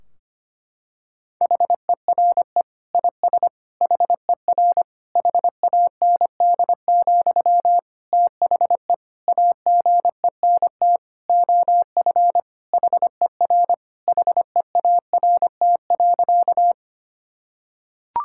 Morse code, heard as a keyed tone, about 25 wpm, 700 Hz. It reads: HERE IS HER HAND, THE AGENT OF HER HEART.
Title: Sentences from Top 1600 Words 25wpm